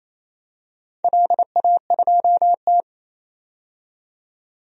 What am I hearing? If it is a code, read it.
Morse code, heard as a keyed tone, about 27 wpm, 700 Hz. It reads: LA2T
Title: DXCC-100 Call Signs - 4 Characters - 28wpm